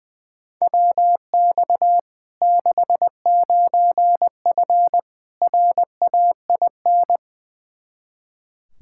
Morse code, heard as a keyed tone, about 21 wpm, 700 Hz. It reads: WX 69F RAIN
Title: Practice QSO - Exchange 2 - Info - 20wpm